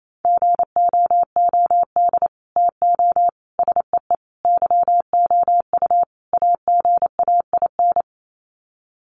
Morse code, heard as a keyed tone, about 28 wpm, 700 Hz. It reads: GOOD TO SEE YOU AGAIN
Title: Sets of 5 Words Encore2 28wpm